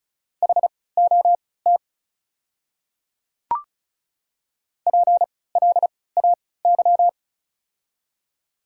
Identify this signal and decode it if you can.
Morse code, heard as a keyed tone, about 36 wpm, spaced as for 20 wpm, 700 Hz. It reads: HOT PLAY
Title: 3x Character Spacing - Top 200 Words - 35wpm